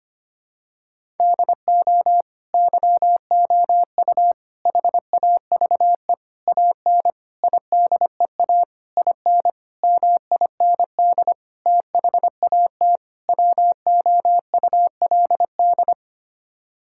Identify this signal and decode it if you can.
Morse code, heard as a keyed tone, about 25 wpm, 700 Hz. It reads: DO YOU HAVE AN IDEA IN MIND THAT WOULD
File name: Sets of 9 Words - 25wpm